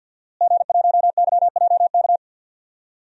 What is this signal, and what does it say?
Morse code, heard as a keyed tone, about 47 wpm, 700 Hz. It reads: G1YJK
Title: DXCC-100 Call Signs - 5 Characters - Repeated 2x - 50wpm